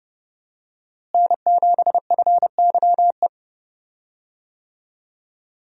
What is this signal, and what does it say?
Morse code, heard as a keyed tone, about 30 wpm, 700 Hz. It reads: N7FYE
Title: USA Call Signs - 5 Characters - Repeated 2x - 30wpm